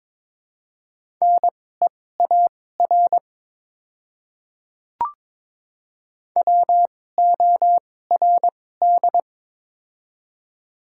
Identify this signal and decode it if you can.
Morse code, heard as a keyed tone, about 22 wpm, spaced as for 16 wpm, 700 Hz. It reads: NEAR WORD
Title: 2x Character Spacing - Top 200 Words - 22wpm